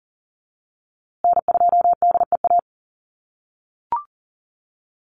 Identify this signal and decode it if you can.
Morse code, heard as a keyed tone, about 40 wpm, 700 Hz. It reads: N2DEA
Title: USA Call Signs Repeated 3x 40wpm